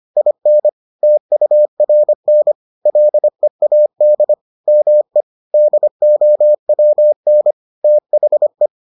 Morse code, heard as a keyed tone, about 25 wpm, 600 Hz. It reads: IN TURN LEAD ME DOWN THE